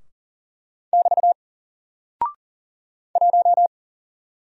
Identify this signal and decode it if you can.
Morse code, heard as a keyed tone, about 40 wpm, 700 Hz. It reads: <BT> 1
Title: Single Character (Letter:Number:Punctuation:Prosign) 40wpm